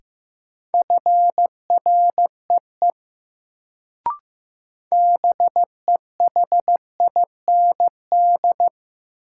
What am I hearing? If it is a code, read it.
Morse code, heard as a keyed tone, about 15 wpm, 700 Hz. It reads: FREE BEHIND
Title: Sentences from Top 500 Words - New Word Drill - 15wpm